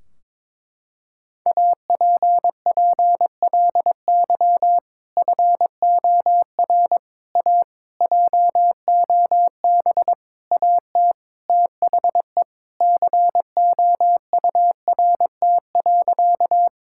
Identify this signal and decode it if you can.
Morse code, heard as a keyed tone, about 22 wpm, 700 Hz. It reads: APPLY FOR A JOB AT THE COURT.
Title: Sentences from Top 1600 Words 22wpm